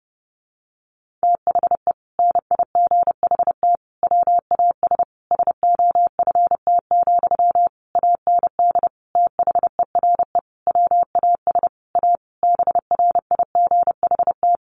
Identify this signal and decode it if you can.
Morse code, heard as a keyed tone, about 30 wpm, 700 Hz. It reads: THE NIGHT WAS SOFT, AND THERE WAS A BRIGHT
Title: Sentences from Top 1500 Words 30wpm